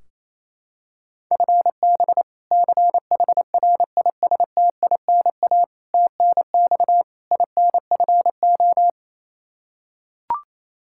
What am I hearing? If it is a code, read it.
Morse code, heard as a keyed tone, about 28 wpm, 700 Hz. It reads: FB CHRISTINA TNX INFO
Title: Practice QSO - Exchange 3 - More Info - 28wpm